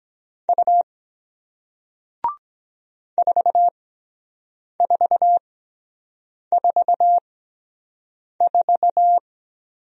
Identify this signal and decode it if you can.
Morse code, heard as a keyed tone, about 23 wpm, 700 Hz. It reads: U 4 4 4 4